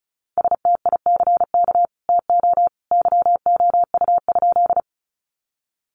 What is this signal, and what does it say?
Morse code, heard as a keyed tone, about 34 wpm, 700 Hz. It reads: STICK TO YOU?
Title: Sentences from Top 1200 Words 35wpm